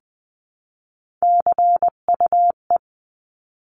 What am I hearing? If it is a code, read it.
Morse code, heard as a keyed tone, about 20 wpm, 700 Hz. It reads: CUE
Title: 259 Lesson 42d - Words 18-20wpm